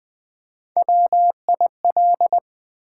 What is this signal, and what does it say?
Morse code, heard as a keyed tone, about 20 wpm, 690 Hz. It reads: WIL
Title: Build A Word - 3 Letters At A Time - 8-Letter Words - 20wpm